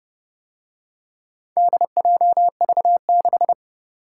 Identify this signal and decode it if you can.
Morse code, heard as a keyed tone, about 30 wpm, 700 Hz. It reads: DJV6